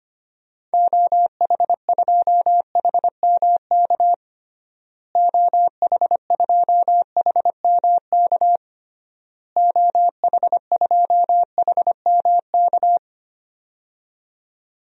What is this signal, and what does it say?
Morse code, heard as a keyed tone, about 25 wpm, 700 Hz. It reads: OH2HMK OH2HMK OH2HMK
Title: DXCC-100 Call Signs - 6 Characters - Repeated 3x - 25wpm